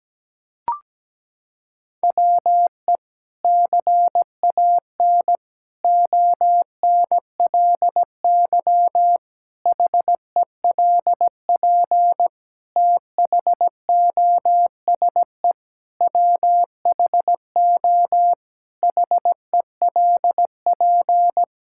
Morse code, heard as a keyed tone, about 17 wpm, 700 Hz. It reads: WE CAN ONLY HELP THOSE WHO HELP
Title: Sentences from Top 1000 Words - Review Part 5 - 17wpm